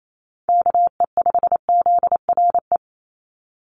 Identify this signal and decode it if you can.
Morse code, heard as a keyed tone, about 28 wpm, 700 Hz. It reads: KE5ZRE
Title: USA Call Signs 28wpm